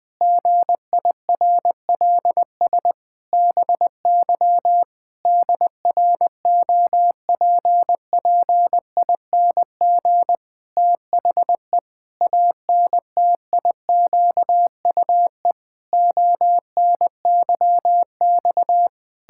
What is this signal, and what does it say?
Morse code, heard as a keyed tone, about 20 wpm, 700 Hz. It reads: GIRLS BY DROPPING THE ANTIQUE ONYX
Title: Pangram Challenge v2 - 20wpm